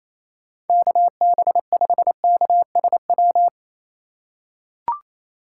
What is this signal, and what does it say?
Morse code, heard as a keyed tone, about 28 wpm, 700 Hz. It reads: KB5KSW
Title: USA Call Signs - 6 Characters - 3x Character Spacing - 28wpm